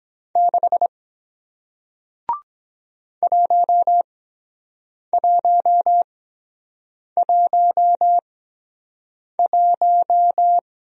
Morse code, heard as a keyed tone, about 23 wpm, spaced as for 7 wpm, 700 Hz. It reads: 6 1111